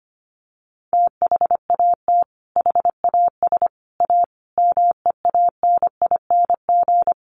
Morse code, heard as a keyed tone, about 25 wpm, 700 Hz. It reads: THAT HAS A MEANING
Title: Sentences from Top 900 Words 25wpm